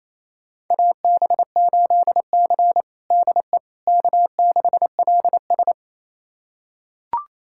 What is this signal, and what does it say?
Morse code, heard as a keyed tone, about 28 wpm, 700 Hz. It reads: AB8C DE K6LS
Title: USA Callsign DE Callsign - 4 Characters - 28wpm